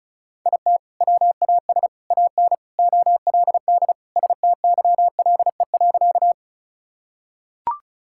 Morse code, heard as a keyed tone, about 36 wpm, 700 Hz. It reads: IT WAS AN OLD STYLE.
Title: Sentences from Top 1000 Words - Review Part 5 - 35wpm